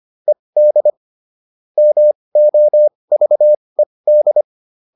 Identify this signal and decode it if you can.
Morse code, heard as a keyed tone, about 25 wpm, spaced as for 20 wpm, 600 Hz. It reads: ED MOVED